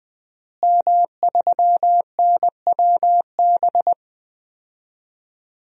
Morse code, heard as a keyed tone, about 21 wpm, 700 Hz. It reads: M3NWB
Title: DXCC-100 Call Signs - 5 Characters - Repeated 3x - 20wpm